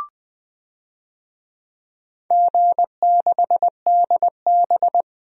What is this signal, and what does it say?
Morse code, heard as a keyed tone, about 20 wpm, 700 Hz. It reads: G6DB
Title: DXCC-100 Call Signs - 4 Characters - 20wpm